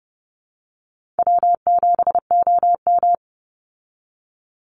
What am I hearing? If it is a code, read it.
Morse code, heard as a keyed tone, about 30 wpm, 700 Hz. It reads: W7OM
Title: Call Signs Repeated 3x - 30wpm